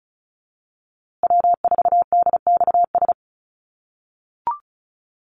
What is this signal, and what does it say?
Morse code, heard as a keyed tone, about 34 wpm, 700 Hz. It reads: W4DXS
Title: USA Call Signs Repeated 2x 35wpm